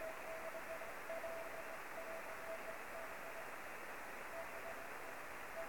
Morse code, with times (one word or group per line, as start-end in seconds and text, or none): none